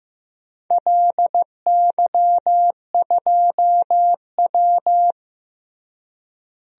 Morse code, heard as a keyed tone, about 15 wpm, 700 Hz. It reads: LY2W
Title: DXCC-100 Call Signs - 4 Characters - Repeated 2x - 15wpm